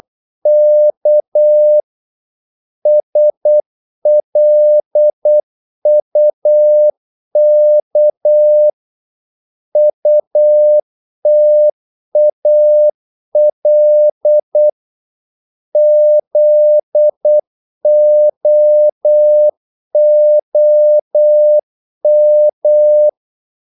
Morse code, heard as a keyed tone, about 8 wpm, 600 Hz. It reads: K SLUK UTAL ZOOM
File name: Korte ord DK 8wpm